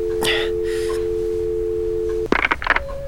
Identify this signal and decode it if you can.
Dial tone (North America), 350 + 440 Hz.